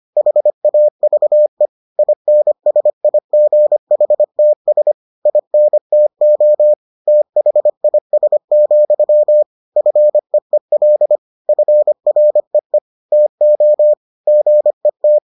Morse code, heard as a keyed tone, about 25 wpm, 600 Hz. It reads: HAVE INSIGHTS INTO THIS, FEEL FREE TO GET